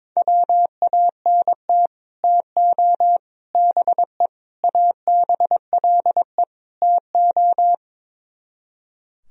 Morse code, heard as a keyed tone, about 22 wpm, 700 Hz. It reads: WANT TO BE ABLE TO
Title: Speed-Racing - Sets of 5 Words 22wpm